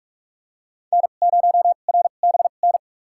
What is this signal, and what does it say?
Morse code, heard as a keyed tone, about 46 wpm, spaced as for 32 wpm, 700 Hz. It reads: N0RBN
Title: USA Call Signs - 5 Characters - 2x Character Spacing - 45wpm